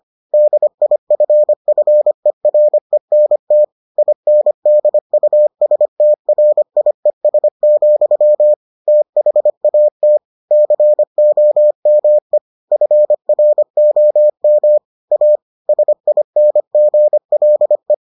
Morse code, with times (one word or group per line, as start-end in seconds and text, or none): 0.3-3.6 DIFFERENT
4.0-8.5 INDUSTRIES,
8.9-10.2 THAT
10.5-12.4 COME
12.7-14.8 FROM
15.1-15.4 A
15.7-17.9 SINGLE